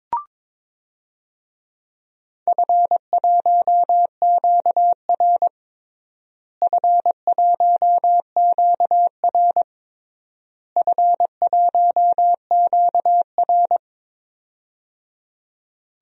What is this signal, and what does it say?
Morse code, heard as a keyed tone, about 22 wpm, 700 Hz. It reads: F1QR F1QR F1QR
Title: DXCC-100 Call Signs - 4 Characters - Repeated 3x - 22wpm